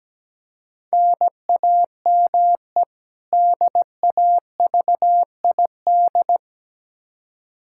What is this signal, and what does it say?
Morse code, heard as a keyed tone, about 17 wpm, 700 Hz. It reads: NAME DAVID
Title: K1USN SST QSO Practice 17wpm